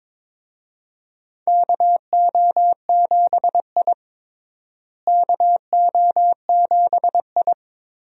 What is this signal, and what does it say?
Morse code, heard as a keyed tone, about 22 wpm, 700 Hz. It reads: KO7I KO7I
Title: USA Call Signs - 4 Characters - Repeated 2x - 22wpm